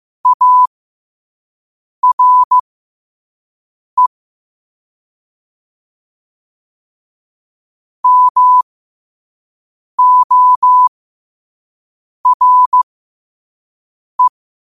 Morse code, heard as a keyed tone, about 15 wpm, spaced as for 5 wpm, 1000 Hz. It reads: ARE MORE
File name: qotdfiles / 5wpm